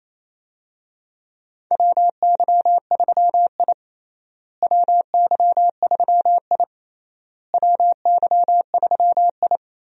Morse code, heard as a keyed tone, about 28 wpm, 700 Hz. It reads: WY3I WY3I WY3I
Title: USA Call Signs - 4 Characters - Repeated 3x - 28wpm